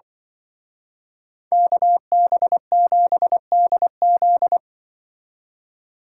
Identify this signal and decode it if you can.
Morse code, heard as a keyed tone, about 23 wpm, 700 Hz. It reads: KB7DZ
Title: Speed Racing - USA 2x2 Call Signs 20wpm